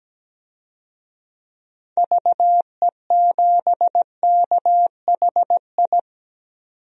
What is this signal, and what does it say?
Morse code, heard as a keyed tone, about 17 wpm, 700 Hz. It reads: VE7KHI
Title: DXCC-100 Call Signs - 6 Characters - 17wpm